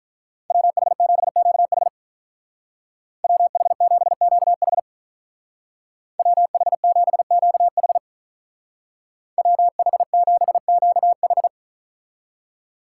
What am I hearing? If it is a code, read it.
Morse code, heard as a keyed tone, about 42 wpm, 700 Hz. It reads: WH7QH WH7QH WH7QH WH7QH
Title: Speed Racing - USA 2x2 Call Signs 35wpm